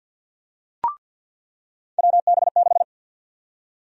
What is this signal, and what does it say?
Morse code, heard as a keyed tone, about 49 wpm, 700 Hz. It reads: WB6
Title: USA Call Sign Prefixes - 50wpm